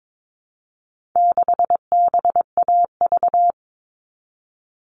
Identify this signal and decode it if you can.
Morse code, heard as a keyed tone, about 22 wpm, 700 Hz. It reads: 6BAV
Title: All Cut Numbers - 4-Digit 22wpm